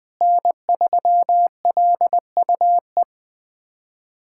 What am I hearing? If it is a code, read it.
Morse code, heard as a keyed tone, about 20 wpm, 700 Hz. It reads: N3LUE
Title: USA Callsign DE Callsign - 5 Characters - 20wpm